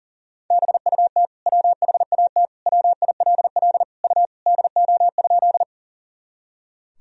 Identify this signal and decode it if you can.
Morse code, heard as a keyed tone, about 40 wpm, 700 Hz. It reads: BUT WHAT WILL U DO?
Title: Sentences from Top 100 Words and Abbreviations 40wpm